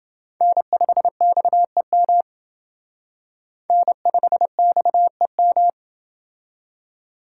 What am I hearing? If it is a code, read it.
Morse code, heard as a keyed tone, about 27 wpm, 700 Hz. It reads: N5XEM N5XEM